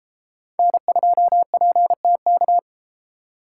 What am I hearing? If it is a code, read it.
Morse code, heard as a keyed tone, about 31 wpm, 700 Hz. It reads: N2PTK